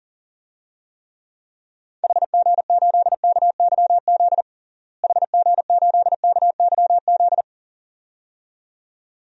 Morse code, heard as a keyed tone, about 40 wpm, 700 Hz. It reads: HG8KYZ HG8KYZ
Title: DXCC-100 Call Signs - 6 Characters - Repeated 2x - 40wpm